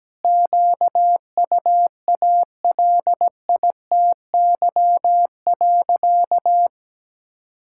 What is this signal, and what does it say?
Morse code, heard as a keyed tone, about 17 wpm, 700 Hz. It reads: QUALITY.
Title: Sentences from Top 1000 Words - Review Part 5 - 17wpm